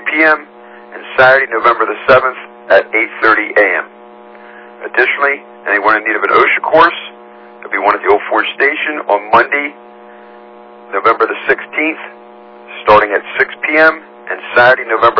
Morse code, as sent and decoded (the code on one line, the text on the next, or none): none